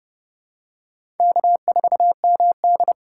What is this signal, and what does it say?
Morse code, heard as a keyed tone, about 30 wpm, 700 Hz. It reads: K4MD
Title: USA Call Signs - 4 Characters - 4x Character Spacing - 30wpm